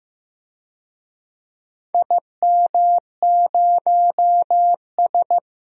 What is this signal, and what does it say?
Morse code, heard as a keyed tone, about 15 wpm, 700 Hz. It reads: IM0S
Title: DXCC-100 Call Signs - 4 Characters - 15wpm